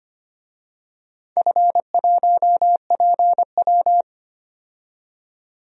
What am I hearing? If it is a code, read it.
Morse code, heard as a keyed tone, about 25 wpm, 700 Hz. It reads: F1PW
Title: DXCC-100 Call Signs - 4 Characters - 25wpm